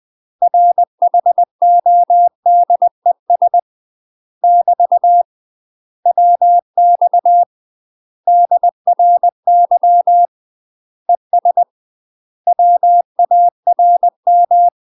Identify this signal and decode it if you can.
Morse code, heard as a keyed tone, about 20 wpm, 700 Hz. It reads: RHODES <BT> WX DRY ES WARM
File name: UK MASTER QSOs TRAINING / 20WPM / UK MASTER QSOs 0022